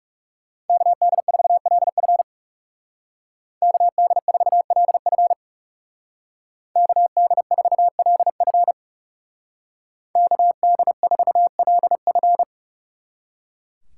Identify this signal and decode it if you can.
Morse code, heard as a keyed tone, about 36 wpm, 700 Hz. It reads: KD4LF KD4LF KD4LF KD4LF
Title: Speed Racing - USA 2x2 Call Signs 30wpm